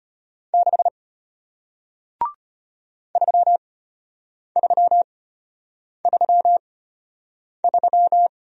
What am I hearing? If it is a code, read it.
Morse code, heard as a keyed tone, about 33 wpm, spaced as for 8 wpm, 700 Hz. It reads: 6 3333